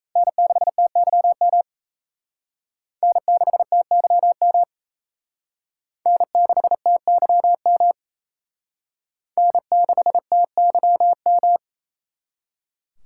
Morse code, heard as a keyed tone, about 38 wpm, 700 Hz. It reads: N6TYM N6TYM N6TYM N6TYM